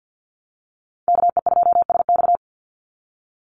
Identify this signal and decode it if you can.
Morse code, heard as a keyed tone, about 50 wpm, 700 Hz. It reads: KE2SX
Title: Call Signs Repeated 2x - 50wpm